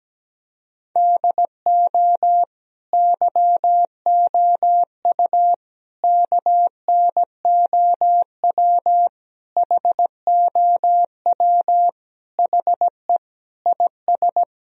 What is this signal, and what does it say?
Morse code, heard as a keyed tone, about 17 wpm, 700 Hz. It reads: DO YOU KNOW HOW HE IS
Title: Sentences from Top 1000 Words - Review Part 3 - 17wpm